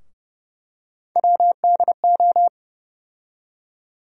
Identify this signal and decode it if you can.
Morse code, heard as a keyed tone, about 30 wpm, 700 Hz. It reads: WDO
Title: USA Call Sign Postfixes - 3 Letters 30wpm